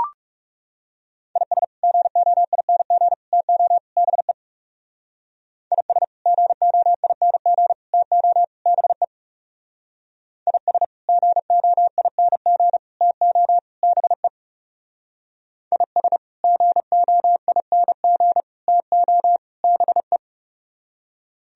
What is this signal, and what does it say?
Morse code, heard as a keyed tone, about 36 wpm, 700 Hz. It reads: IS GOING TO BE IS GOING TO BE IS GOING TO BE IS GOING TO BE
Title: Speed-Racing - Sets of 4 Words 30wpm